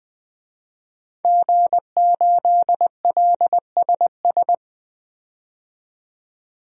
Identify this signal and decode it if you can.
Morse code, heard as a keyed tone, about 20 wpm, 700 Hz. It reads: G8LSS
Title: DXCC-100 Call Signs - 5 Characters - 20wpm